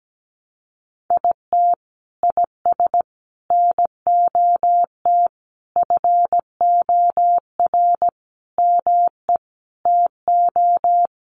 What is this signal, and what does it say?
Morse code, heard as a keyed tone, about 17 wpm, 700 Hz. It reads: IT IS NOT FOR ME TO
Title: Sentences from Top 600 Words 17wpm